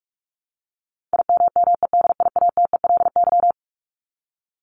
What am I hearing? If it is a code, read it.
Morse code, heard as a keyed tone, about 44 wpm, 700 Hz. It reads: IMMEDIATELY
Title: Top 1001-1250 Words 45wpm